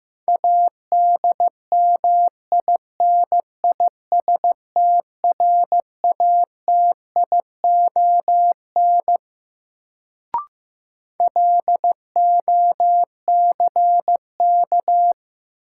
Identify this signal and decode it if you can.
Morse code, heard as a keyed tone, about 15 wpm, 700 Hz. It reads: ADMINISTRATION LOCK
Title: Sentences from Top 1300 Words - New Word Drill Alt - 15wpm